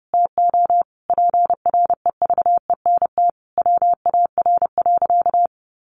Morse code, heard as a keyed tone, about 30 wpm, 700 Hz. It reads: TO PREVENT WAR.
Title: Sentences from Top 1100 Words 30wpm